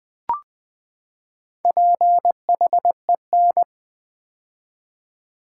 Morse code, heard as a keyed tone, about 20 wpm, 700 Hz. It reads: PHEN
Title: Build A Word - 4 Letters At A Time - 10-Letter Words - 20wpm